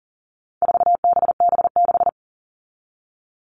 Morse code, heard as a keyed tone, about 40 wpm, 700 Hz. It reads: 4BB6
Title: All Cut Numbers - 4-Digit 40wpm